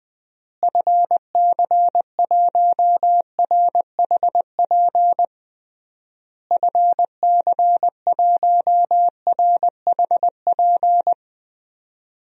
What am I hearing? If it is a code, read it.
Morse code, heard as a keyed tone, about 20 wpm, 700 Hz. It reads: FC1RHP FC1RHP
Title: DXCC-100 Call Signs - 6 Characters - Repeated 3x - 20wpm